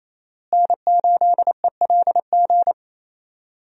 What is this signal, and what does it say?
Morse code, heard as a keyed tone, about 28 wpm, 700 Hz. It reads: N8ELG
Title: USA Call Signs - 5 Characters - Repeated 3x - 28wpm